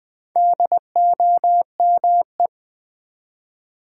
Morse code, heard as a keyed tone, about 20 wpm, 700 Hz. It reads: DOME
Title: Build A Word - 2 Letters At A Time - 8-Letter Words - 20wpm